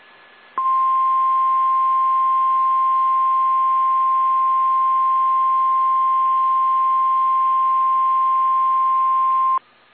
Emergency Alert System (SAME) tones (the 1050 Hz attention tone). It is heard from 0.6 to 9.6 s.